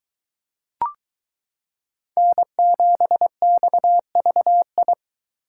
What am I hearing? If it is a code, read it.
Morse code, heard as a keyed tone, about 23 wpm, 700 Hz. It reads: N7XVI